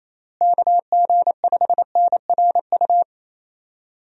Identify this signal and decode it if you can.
Morse code, heard as a keyed tone, about 28 wpm, 700 Hz. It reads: KG5NRU
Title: USA Call Signs - 6 Characters - Repeated 3x - 28wpm